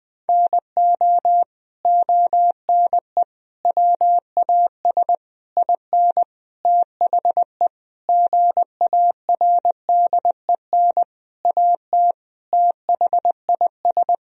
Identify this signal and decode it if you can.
Morse code, heard as a keyed tone, about 20 wpm, 700 Hz. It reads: NO ONE WAS IN THE GARDEN AT THIS
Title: Sentences from Top 1000 Words - Review Part 4 - 20wpm